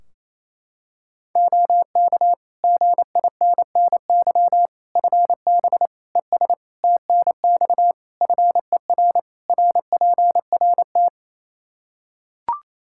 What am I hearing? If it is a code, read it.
Morse code, heard as a keyed tone, about 28 wpm, 700 Hz. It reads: OK GINNY FB ES TNX FER RPRT
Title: Practice QSO - Exchange 2 - Info - 28wpm